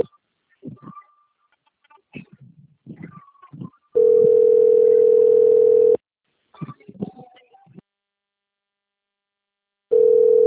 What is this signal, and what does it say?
Ringback tone (North America), 440 + 480 Hz.